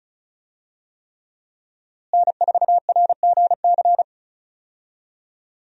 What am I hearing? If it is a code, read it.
Morse code, heard as a keyed tone, about 36 wpm, 700 Hz. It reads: N4RGC
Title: USA Call Signs - 5 Characters - Repeated 3x - 35wpm